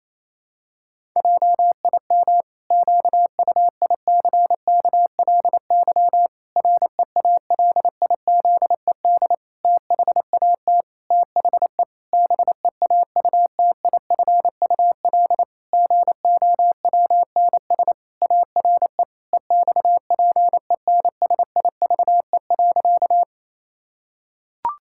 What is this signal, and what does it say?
Morse code, heard as a keyed tone, about 28 wpm, 700 Hz. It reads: JIM QUICKLY REALIZED THAT THE BEAUTIFUL GOWNS ARE EXPENSIVE.
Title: Pangram Challenge v2 - 28wpm